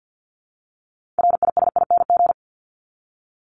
Morse code, heard as a keyed tone, about 50 wpm, 700 Hz. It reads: RISING
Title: ING Suffix - Top 100 Words with Base Words 50wpm